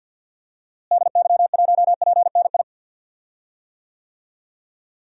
Morse code, heard as a keyed tone, about 53 wpm, 700 Hz. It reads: DY1PNI
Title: DXCC-100 Call Signs - 6 Characters - Repeated 2x - 50wpm